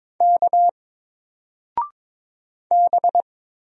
Morse code, heard as a keyed tone, about 22 wpm, 700 Hz. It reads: K B
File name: Single-Letter-Number-22wpm